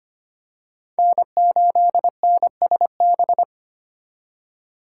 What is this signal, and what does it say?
Morse code, heard as a keyed tone, about 25 wpm, 700 Hz. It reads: N8NSB
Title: USA Call Signs - 5 Characters - 3x Character Spacing - 25wpm